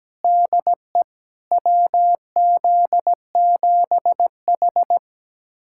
Morse code, heard as a keyed tone, about 17 wpm, 700 Hz. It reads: DE WZ7H
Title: USA Callsign DE Callsign - 4 Characters - 17wpm